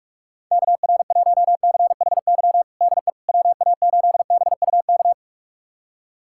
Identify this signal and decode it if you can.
Morse code, heard as a keyed tone, about 46 wpm, 690 Hz. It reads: KR1CHY DE WA8BUK
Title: USA Callsign DE Callsign - 6 Characters - 45wpm